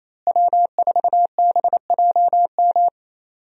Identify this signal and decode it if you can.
Morse code, heard as a keyed tone, about 27 wpm, 700 Hz. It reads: W4BJM
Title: USA Call Signs - 5 Characters - Repeated 3x - 28wpm